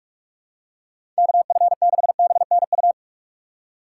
Morse code, heard as a keyed tone, about 46 wpm, 700 Hz. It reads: KF6BNU
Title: USA Call Signs - 6 Characters - 2x Character Spacing - 45wpm